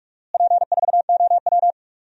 Morse code, heard as a keyed tone, about 46 wpm, 700 Hz. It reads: P4OW
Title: DXCC-100 Call Signs - 4 Characters - Repeated 2x - 45wpm